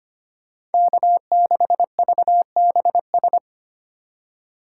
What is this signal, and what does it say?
Morse code, heard as a keyed tone, about 30 wpm, 700 Hz. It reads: K6VBS
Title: Speed Racing - USA 1x3 Call Signs 25wpm